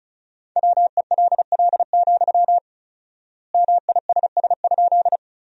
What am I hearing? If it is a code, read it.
Morse code, heard as a keyed tone, about 36 wpm, 700 Hz. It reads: WELL, MISS?
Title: 4x Word-Spacing - Sentences from Top 500 Words - 35wpm